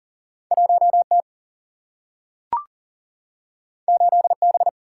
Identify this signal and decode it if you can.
Morse code, heard as a keyed tone, about 40 wpm, 700 Hz. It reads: 1 T 8 B
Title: Single-Letter Single-Number 40wpm